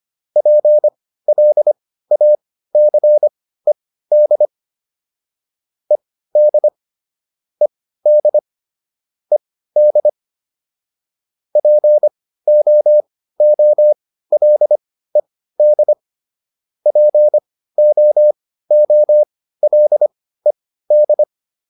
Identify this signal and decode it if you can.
Morse code, heard as a keyed tone, about 25 wpm, spaced as for 15 wpm, 600 Hz. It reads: PLACED ED ED ED POOLED POOLED